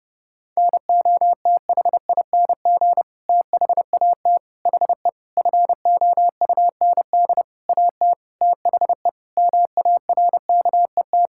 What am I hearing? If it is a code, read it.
Morse code, heard as a keyed tone, about 30 wpm, 700 Hz. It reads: NOTHING THAT HE FOUND AT THE MARKET
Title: Sentences from Top 1000 Words - Review Part 6 - 30wpm